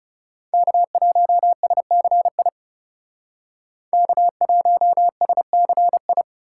Morse code, heard as a keyed tone, about 34 wpm, 700 Hz. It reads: K1SCI K1SCI
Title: Speed Racing - USA 1x3 Call Signs 30wpm